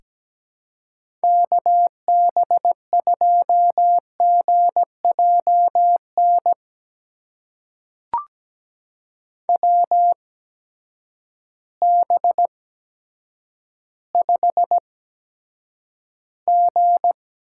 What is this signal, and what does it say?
Morse code, heard as a keyed tone, about 17 wpm, spaced as for 9 wpm, 700 Hz. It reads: KB2GJN WB5G
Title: USA Call Signs - 6 Characters - 8x Character Spacing - 17wpm